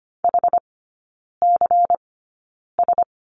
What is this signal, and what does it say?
Morse code, heard as a keyed tone, about 25 wpm, spaced as for 9 wpm, 700 Hz. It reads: HCS
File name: course / 25:10wpm / 093 Lesson 16e - Letter c - Three Characters 25wpm